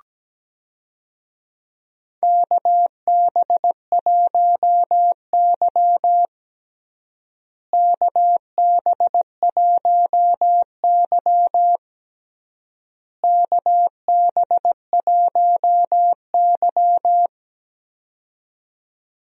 Morse code, heard as a keyed tone, about 17 wpm, 700 Hz. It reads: KB1Y KB1Y KB1Y
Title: USA Call Signs - 4 Characters - Repeated 3x - 17wpm